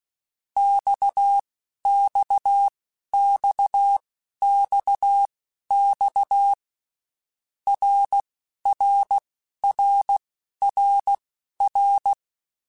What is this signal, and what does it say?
Morse code, heard as a keyed tone, about 16 wpm, spaced as for 11 wpm, 800 Hz. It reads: XXXXX RRRRR
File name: Basic Letter Groups By7Groups / RKPX / RKPX@10